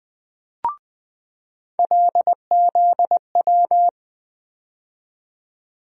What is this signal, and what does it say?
Morse code, heard as a keyed tone, about 20 wpm, 700 Hz. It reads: LZW
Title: USA Call Sign Postfixes - 3 Letters 20wpm